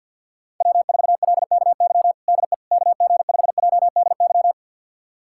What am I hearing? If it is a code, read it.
Morse code, heard as a keyed tone, about 46 wpm, 700 Hz. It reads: W4LKY DE KG5JDY
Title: USA Callsign DE Callsign 50wpm